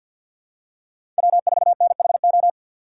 Morse code, heard as a keyed tone, about 53 wpm, 700 Hz. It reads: W4NHO
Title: USA Call Signs - 5 Characters - 3x Character Spacing - 50wpm